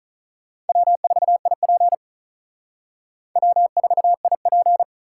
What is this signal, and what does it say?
Morse code, heard as a keyed tone, about 39 wpm, 700 Hz. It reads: W4IP W4IP
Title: Speed Racing - USA 1x2 Call Signs 35wpm